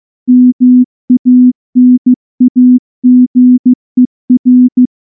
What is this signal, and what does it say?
Morse code, heard as a keyed tone, about 14 wpm, 250 Hz. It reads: MANAGER
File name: morse / morse-training-15WPM